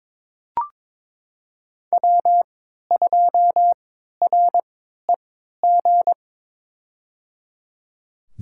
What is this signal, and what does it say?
Morse code, heard as a keyed tone, about 22 wpm, spaced as for 12 wpm, 700 Hz. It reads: W2REG
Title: USA Call Signs - 3x Character Spacing - 22wpm